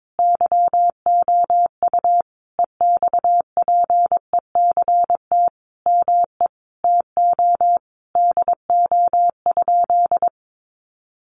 Morse code, heard as a keyed tone, about 22 wpm, 700 Hz. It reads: YOU EXPECT ME TO DO?
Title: Sentences from Top 1100 Words 22wpm